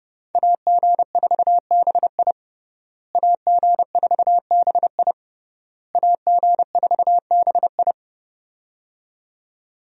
Morse code, heard as a keyed tone, about 30 wpm, 700 Hz. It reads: AG4BI AG4BI AG4BI
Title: USA Call Signs - 5 Characters - Repeated 3x - 30wpm